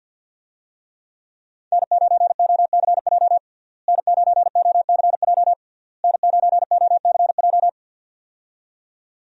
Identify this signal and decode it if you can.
Morse code, heard as a keyed tone, about 52 wpm, 700 Hz. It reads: N9OCJ N9OCJ N9OCJ
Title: USA Call Signs - 5 Characters - Repeated 3x - 50wpm